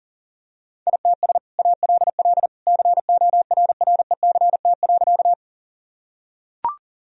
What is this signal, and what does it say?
Morse code, heard as a keyed tone, about 40 wpm, 700 Hz. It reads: ITS ALL CORRECT.
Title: Sentences from Top 1000 Words - Review Part 2 - 40wpm